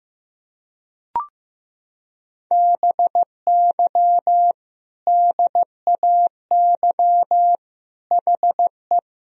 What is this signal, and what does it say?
Morse code, heard as a keyed tone, about 15 wpm, 700 Hz. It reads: BY DAY HE
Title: Sentences from Top 1000 Words - Review Part 1 - 15wpm